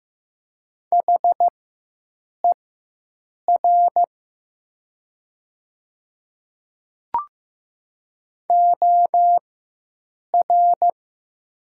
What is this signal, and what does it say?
Morse code, heard as a keyed tone, about 15 wpm, spaced as for 7 wpm, 700 Hz. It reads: HER OR
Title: 4x Character Spacing - Top 300 Words - 15wpm